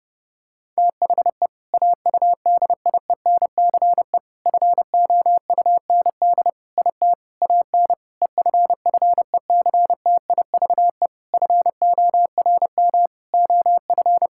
Morse code, heard as a keyed tone, about 30 wpm, 700 Hz. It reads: THE AUDIENCE FOUND IT AN EFFECTIVE FORM OF
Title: Sentences from Top 1000 Words - Review Part 6 - 30wpm